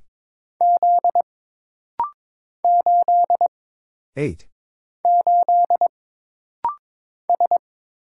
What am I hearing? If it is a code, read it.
Morse code, heard as a keyed tone, about 22 wpm, 700 Hz. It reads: Z 8 8 S
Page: This rapid-fire timing is geared to push your ICR (Instant Character Recognition) skills.